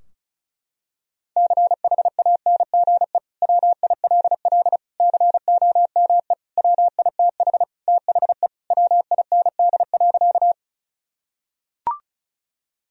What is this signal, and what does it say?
Morse code, heard as a keyed tone, about 36 wpm, 700 Hz. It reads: CHANGE WILL COME WITH THE WIND.
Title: Sentences from Top 1000 Words - Review Part 2 - 35wpm